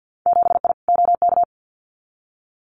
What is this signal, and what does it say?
Morse code, heard as a keyed tone, about 50 wpm, 700 Hz. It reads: 7I OK
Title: Winter Field Day Exchanges Encore 50wpm